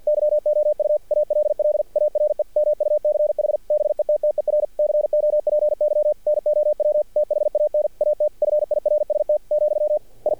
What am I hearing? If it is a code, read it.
Morse code, heard as a keyed tone, about 50 wpm, 600 Hz. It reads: YOU ALL ARE MUCH BETTER COPY NOW THAN AT FIRST , I
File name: archived / CRSnet